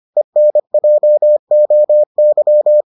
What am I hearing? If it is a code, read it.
Morse code, heard as a keyed tone, about 25 wpm, 600 Hz. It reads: ENJOY